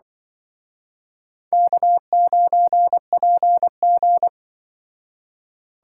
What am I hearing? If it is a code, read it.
Morse code, heard as a keyed tone, about 23 wpm, 700 Hz. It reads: K9PG
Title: Speed Racing - USA 1x2 Call Signs 20wpm